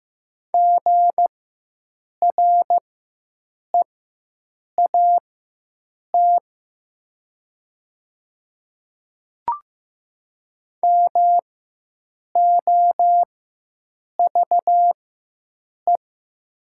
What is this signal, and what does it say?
Morse code, heard as a keyed tone, about 15 wpm, spaced as for 7 wpm, 700 Hz. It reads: GREAT MOVE
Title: 4x Character Spacing - Top 200 Words - 15wpm